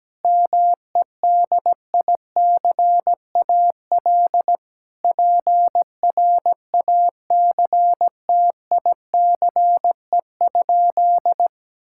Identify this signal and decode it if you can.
Morse code, heard as a keyed tone, about 17 wpm, 700 Hz. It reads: MEDICAL PRACTICE?
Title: Sentences from Top 1000 Words - Review Part 5 - 17wpm